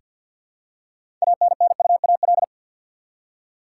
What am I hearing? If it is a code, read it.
Morse code, heard as a keyed tone, about 49 wpm, 700 Hz. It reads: ANNUAL
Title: Six Letter Words - 50wpm